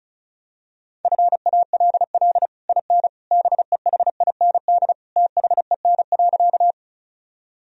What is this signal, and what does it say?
Morse code, heard as a keyed tone, about 36 wpm, 700 Hz. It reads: FALL IN BEHIND THEN.
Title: Sentences from Top 1000 Words - Review Part 2 - 35wpm